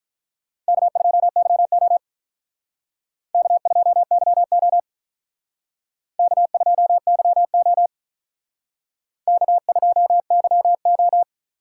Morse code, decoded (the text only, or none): K2YO K2YO K2YO K2YO